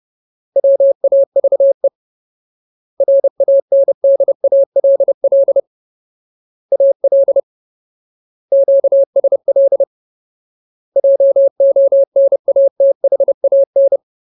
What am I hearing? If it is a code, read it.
Morse code, heard as a keyed tone, about 30 wpm, 550 Hz. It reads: WAVE RANDALL AL QSL JONATHAN
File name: Ultimate Word List / 30WPM Ultimate